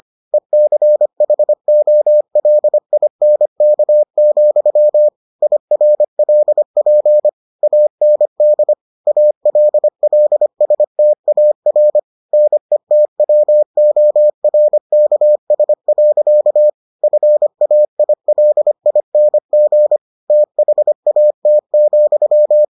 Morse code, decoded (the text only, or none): ECHOLINK, IRLP AND ALLSTAR NETWORKS. FAILING THAT,